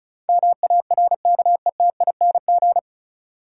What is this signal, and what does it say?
Morse code, heard as a keyed tone, about 36 wpm, 700 Hz. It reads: MARKETING
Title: Sentences from Top 1000 Words - Review Part 5 - New Word Drill - 35wpm